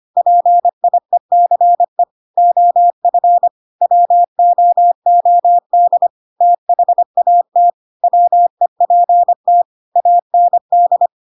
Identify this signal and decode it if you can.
Morse code, heard as a keyed tone, about 25 wpm, 700 Hz. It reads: PIECE OF WOOD THAT WEPT AND
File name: Pinocchio / Pinocchio ch1 25 25 ik1hge